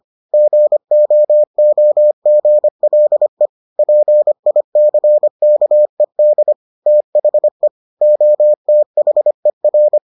Morse code, heard as a keyed tone, about 25 wpm, 600 Hz. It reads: GOOGLE PICKED THE OTHER